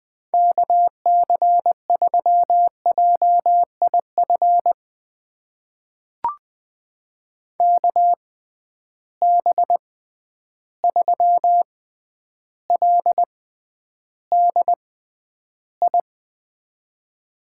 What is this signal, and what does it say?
Morse code, heard as a keyed tone, about 20 wpm, spaced as for 10 wpm, 700 Hz. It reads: KC3JIF KB3LDI
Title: USA Call Signs - 6 Characters - 6x Character Spacing - 20wpm